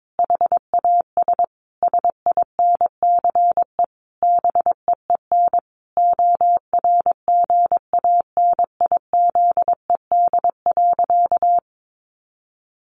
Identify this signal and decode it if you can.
Morse code, heard as a keyed tone, about 22 wpm, 700 Hz. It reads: HAS SINCE BEEN ORGANIZED.
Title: Sentences from Top 1400 Words 22wpm